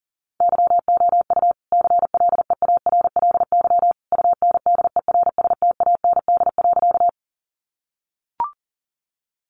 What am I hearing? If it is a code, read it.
Morse code, heard as a keyed tone, about 40 wpm, 700 Hz. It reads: YOU CLEARLY UNDERSTAND.
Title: Sentences from Top 1300 Words 40wpm